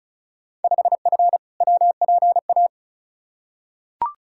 Morse code, heard as a keyed tone, about 36 wpm, 700 Hz. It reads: 5F WPA
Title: Field Day Exchanges Encore - 35wpm